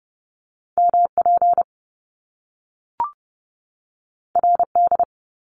Morse code, heard as a keyed tone, about 30 wpm, 700 Hz. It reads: MP RD
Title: Top Two Letter Combinations in Words 30wpm